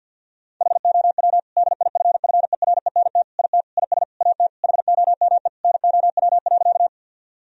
Morse code, heard as a keyed tone, about 50 wpm, 700 Hz. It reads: HOW DIFFERENT IT IS AT HOME NOW.
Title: Sentences from Top 1000 Words - Review Part 3 - 50wpm